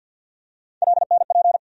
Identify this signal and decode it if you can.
Morse code, heard as a keyed tone, about 52 wpm, 700 Hz. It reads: LNP
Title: USA Call Sign Postfixes - 3 Letters 50wpm